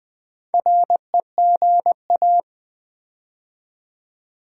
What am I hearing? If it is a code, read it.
Morse code, heard as a keyed tone, about 20 wpm, 700 Hz. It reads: REGA
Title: Build A Word - 4 Letters At A Time - 10-Letter Words - 20wpm